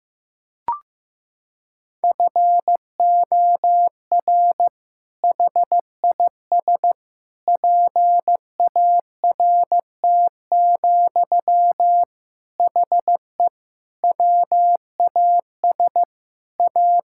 Morse code, heard as a keyed tone, about 15 wpm, 700 Hz. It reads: FOR HIS PART, HE WAS A
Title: Sentences from Top 1000 Words - Review Part 5 - 15wpm